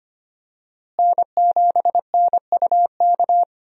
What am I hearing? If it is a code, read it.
Morse code, heard as a keyed tone, about 25 wpm, 700 Hz. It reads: N7NUK
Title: USA Call Signs - 5 Characters - Repeated 3x - 25wpm